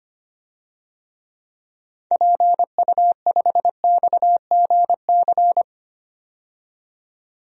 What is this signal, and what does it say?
Morse code, heard as a keyed tone, about 25 wpm, 700 Hz. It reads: PU5XGC
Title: DXCC-100 Call Signs - 6 Characters - 25wpm